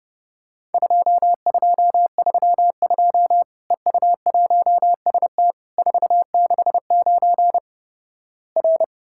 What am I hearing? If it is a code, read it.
Morse code, heard as a keyed tone, about 30 wpm, 700 Hz. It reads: 2232 EU1ST 469 R
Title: Worked All Europe DX Contest - QTC Exchange - 30wpm